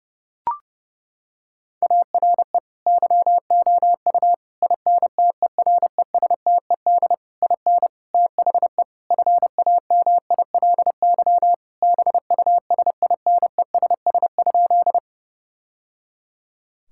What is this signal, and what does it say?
Morse code, heard as a keyed tone, about 32 wpm, 700 Hz. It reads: ARE YOU INTERESTED IN THE FAMILY BUSINESS?
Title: Sentences from Top 1000 Words - Review Part 6 - 30wpm